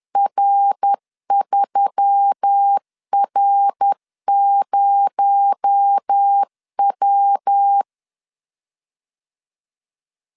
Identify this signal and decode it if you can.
Morse code, heard as a keyed tone, about 10 wpm, 800 Hz. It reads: R3R0W
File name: CW / DET / 1 6 1500C S B 10wpm 4137